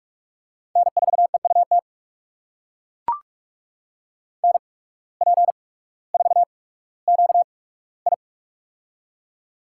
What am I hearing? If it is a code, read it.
Morse code, heard as a keyed tone, about 44 wpm, spaced as for 19 wpm, 690 Hz. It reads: N4EUT NP4QI
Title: USA Call Signs - 5 Characters - 8x Character Spacing - 45wpm